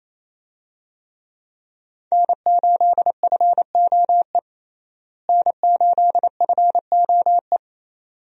Morse code, heard as a keyed tone, about 28 wpm, 700 Hz. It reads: N8FOE N8FOE
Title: USA Call Signs - 5 Characters - Repeated 2x - 28wpm